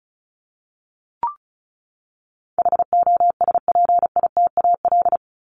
Morse code, heard as a keyed tone, about 34 wpm, 700 Hz. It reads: HOSPITAL
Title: Top 1751-2000 Words 35wpm